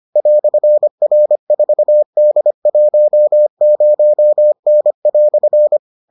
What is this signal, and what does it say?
Morse code, heard as a keyed tone, about 25 wpm, 600 Hz. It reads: "R4D10N"